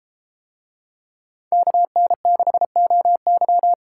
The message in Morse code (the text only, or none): KN6OY